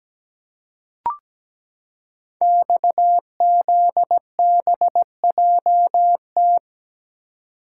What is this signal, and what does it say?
Morse code, heard as a keyed tone, about 17 wpm, 700 Hz. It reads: XZBJT
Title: Code Groups - 5 Letter:Numbers - 17wpm